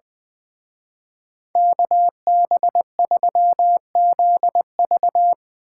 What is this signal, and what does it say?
Morse code, heard as a keyed tone, about 22 wpm, 700 Hz. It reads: KB3ZV
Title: Speed Racing - USA 2x2 Call Signs 20wpm